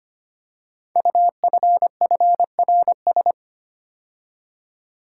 Morse code, heard as a keyed tone, about 25 wpm, 700 Hz. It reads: UFFRS
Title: Code Groups - 5 Letters - 25wpm